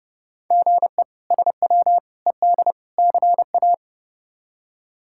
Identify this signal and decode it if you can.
Morse code, heard as a keyed tone, about 30 wpm, 700 Hz. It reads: GE SW ED CA
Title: K1USN SST QSO Practice - 30wpm